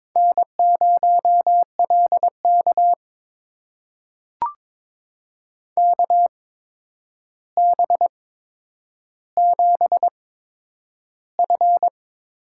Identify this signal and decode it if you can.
Morse code, heard as a keyed tone, about 22 wpm, spaced as for 9 wpm, 700 Hz. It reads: N0LK KB7F
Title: USA Call Signs - 4 Characters - 8x Character Spacing - 22wpm